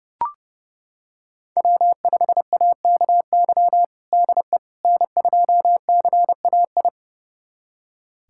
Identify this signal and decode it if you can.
Morse code, heard as a keyed tone, about 30 wpm, 700 Hz. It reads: W5AKY DE N2CAI
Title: USA Callsign DE Callsign - 5 Characters - 30wpm